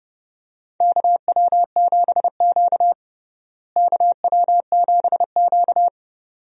Morse code, heard as a keyed tone, about 30 wpm, 700 Hz. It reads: KW7Q KW7Q
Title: USA Call Signs - 4 Characters - Repeated 2x - 30wpm